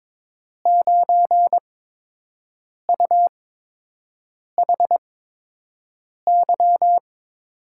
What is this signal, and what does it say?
Morse code, heard as a keyed tone, about 22 wpm, spaced as for 6 wpm, 700 Hz. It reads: 9UHY
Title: USA Call Signs - 5 Characters - 8x Character Spacing - 22wpm